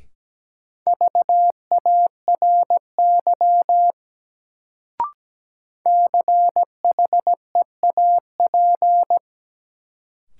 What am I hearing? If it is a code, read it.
Morse code, heard as a keyed tone, about 17 wpm, 700 Hz. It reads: VARY CHEAP
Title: Sentences from Top 1200 Words - New Word Drill Alt - 17wpm